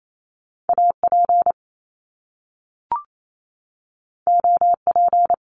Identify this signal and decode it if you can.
Morse code, heard as a keyed tone, about 28 wpm, 700 Hz. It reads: AP OP
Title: Top Two Letter Combinations in Words 28wpm